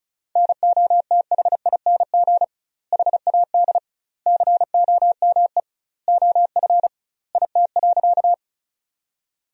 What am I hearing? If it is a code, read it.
Morse code, heard as a keyed tone, about 36 wpm, 700 Hz. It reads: NOTHING HAD COME OF IT.
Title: 2x Word-Spacing - Sentences from Top 500 Words - 35wpm